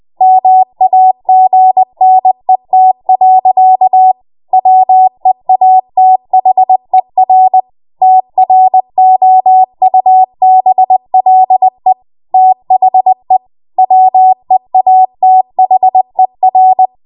Morse code, heard as a keyed tone, about 20 wpm, 750 Hz. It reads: MAGNET. WEATHER TROUBLE THE WEATHER
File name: Morse / 20WPM